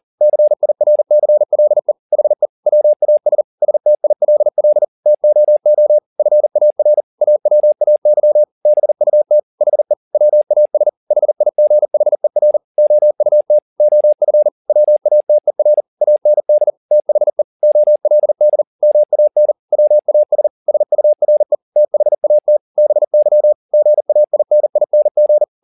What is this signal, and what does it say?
Morse code, heard as a keyed tone, about 40 wpm, 600 Hz. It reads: CIRCLE HE WAS STILL TOO FAR AWAY BUT HE WAS HIGHER OUT OF WATER AND THE OLD MAN WAS SURE THAT BY GAINING
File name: the old man and the sea 2 40wpm